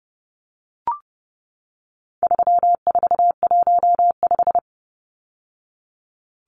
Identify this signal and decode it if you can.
Morse code, heard as a keyed tone, about 30 wpm, 700 Hz. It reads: 3415
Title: Numbers - 4-Digit 30wpm